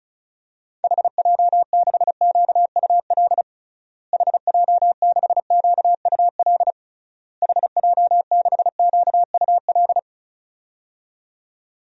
Morse code, heard as a keyed tone, about 36 wpm, 700 Hz. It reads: HJ6QUL HJ6QUL HJ6QUL
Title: DXCC-100 Call Signs - 6 Characters - Repeated 3x - 35wpm